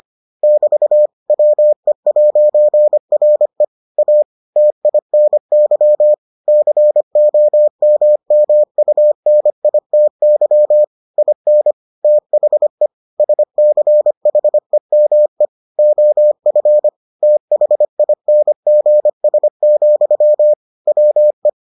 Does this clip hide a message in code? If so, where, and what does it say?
Yes: Morse code from 0.4 to 21.5 s, heard as a keyed tone, about 25 wpm, 600 Hz. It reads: <BT> WE'RE A TINY COMMUNITY IN THE SCHEME OF THINGS, WE